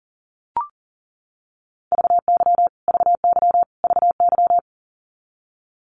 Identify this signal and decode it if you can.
Morse code, heard as a keyed tone, about 40 wpm, 700 Hz. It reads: VY VY VY
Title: Abbreviations Repeated 3x - 40wpm